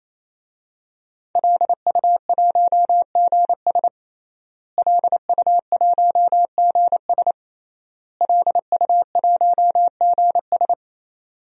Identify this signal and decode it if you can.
Morse code, heard as a keyed tone, about 28 wpm, 700 Hz. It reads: LU1GS LU1GS LU1GS
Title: DXCC-100 Call Signs - 5 Characters - Repeated 3x - 28wpm